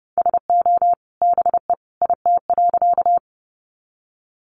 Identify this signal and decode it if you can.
Morse code, heard as a keyed tone, about 30 wpm, 700 Hz. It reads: SO BE IT.
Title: Sentences from Top 100 Words 30wpm